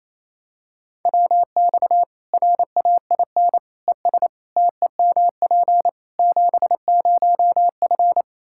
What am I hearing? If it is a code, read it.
Morse code, heard as a keyed tone, about 28 wpm, 700 Hz. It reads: WX RAIN ES TEMP 70F
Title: Practice QSO - Exchange 2 - Info - 28wpm